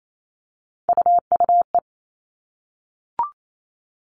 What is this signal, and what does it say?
Morse code, heard as a keyed tone, about 28 wpm, 700 Hz. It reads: UUE
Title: All Cut Numbers - 3-Digit 28wpm